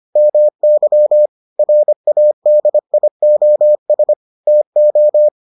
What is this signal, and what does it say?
Morse code, heard as a keyed tone, about 25 wpm, 600 Hz. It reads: MY RADIOS TO